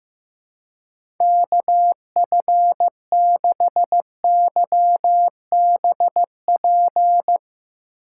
Morse code, heard as a keyed tone, about 15 wpm, 700 Hz. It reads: KF6YBP
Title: USA Call Signs - 6 Characters - 2x Character Spacing - 15wpm